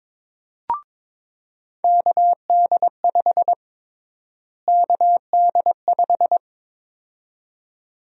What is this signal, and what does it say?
Morse code, heard as a keyed tone, about 22 wpm, 700 Hz. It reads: KD5 KD5
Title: USA Call Sign Prefixes - Repeated 2x - 22wpm